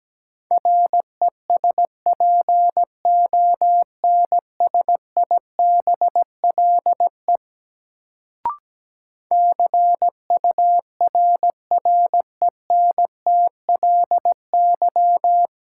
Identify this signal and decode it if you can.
Morse code, heard as a keyed tone, about 17 wpm, 700 Hz. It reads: RESPONSIBLE CURRENTLY
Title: Sentences from Top 1000 Words - Review Part 7 - New Word Drill - 17wpm